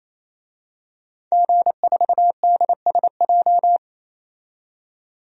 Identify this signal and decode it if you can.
Morse code, heard as a keyed tone, about 27 wpm, 700 Hz. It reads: G4DSJ
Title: DXCC-100 Call Signs - 5 Characters - 28wpm